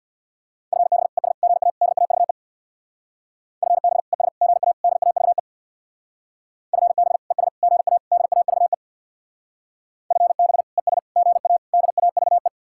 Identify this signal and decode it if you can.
Morse code, heard as a keyed tone, about 60 wpm, 700 Hz. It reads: FB ES GA DAVE FB ES GA DAVE FB ES GA DAVE FB ES GA DAVE
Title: Speed-Racing Common QSO Phrases - 50wpm